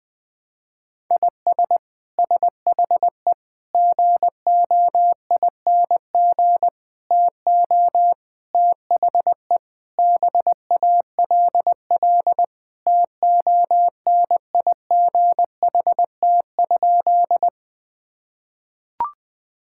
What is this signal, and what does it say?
Morse code, heard as a keyed tone, about 20 wpm, 700 Hz. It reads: IS SHE GOING TO THE BALL TONIGHT?
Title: Sentences from Top 1000 Words - Review Part 6 - 20wpm